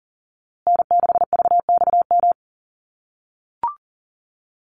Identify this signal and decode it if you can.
Morse code, heard as a keyed tone, about 40 wpm, 700 Hz. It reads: N6VXM
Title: USA Call Signs Repeated 3x 40wpm